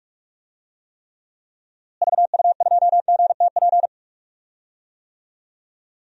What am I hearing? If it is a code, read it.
Morse code, heard as a keyed tone, about 43 wpm, 700 Hz. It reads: VU2GTP
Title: DXCC-100 Call Signs - 6 Characters - Repeated 3x - 45wpm